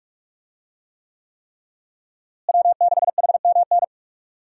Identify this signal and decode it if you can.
Morse code, heard as a keyed tone, about 45 wpm, 700 Hz. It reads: W6HMN
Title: USA Call Signs - 5 Characters - Repeated 3x - 45wpm